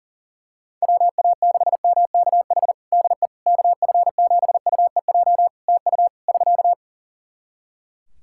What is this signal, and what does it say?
Morse code, heard as a keyed tone, about 40 wpm, 700 Hz. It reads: WA6MKH DE KF7UEJ TU <SK>
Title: K1USN SST QSO Practice 40wpm